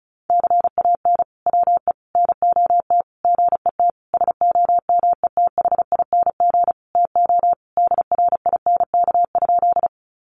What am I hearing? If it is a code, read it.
Morse code, heard as a keyed tone, about 34 wpm, 700 Hz. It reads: CAN WE NOT GET SOMETHING TO DRINK?
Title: Sentences from Top 1500 Words 35wpm